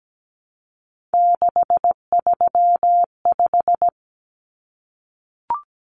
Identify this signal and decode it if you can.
Morse code, heard as a keyed tone, about 17 wpm, 700 Hz. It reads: 635
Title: Common Cut Numbers - 3-Digit 17wpm